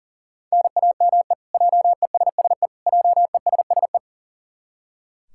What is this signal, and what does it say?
Morse code, heard as a keyed tone, about 38 wpm, 700 Hz. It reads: NAME JESSE JESSE
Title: K1USN SST QSO Practice 40wpm